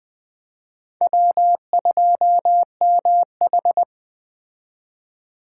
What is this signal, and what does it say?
Morse code, heard as a keyed tone, about 21 wpm, 700 Hz. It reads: W2MH
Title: Speed Racing - USA 1x2 Call Signs 17wpm